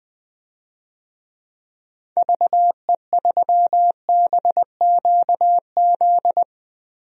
Morse code, heard as a keyed tone, about 20 wpm, 700 Hz. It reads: VE3BQZ
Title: DXCC-100 Call Signs - 6 Characters - Repeated 3x - 20wpm